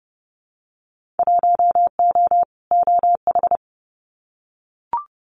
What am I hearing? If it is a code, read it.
Morse code, heard as a keyed tone, about 30 wpm, 700 Hz. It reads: 1O OH
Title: Winter Field Day Exchanges 30wpm